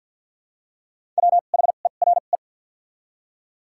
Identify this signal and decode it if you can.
Morse code, heard as a keyed tone, about 46 wpm, 700 Hz. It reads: W H E R E
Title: 2x Character Spacing - Top 200 Words - 50wpm